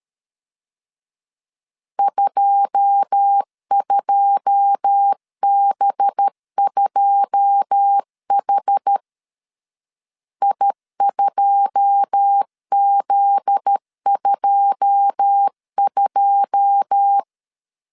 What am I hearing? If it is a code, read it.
Morse code, heard as a keyed tone, about 13 wpm, 800 Hz. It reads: 22B2H I2Z22
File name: LEZIONI / 9 2 12wpm 2301